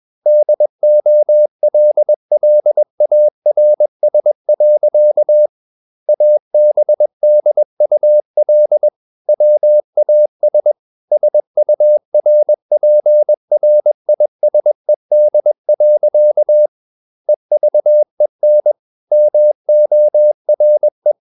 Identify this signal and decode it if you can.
Morse code, heard as a keyed tone, about 21 wpm, 600 Hz. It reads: DOLLARS. ABDUL WAS SURPRISED. EVEN MORE
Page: CW med Gnister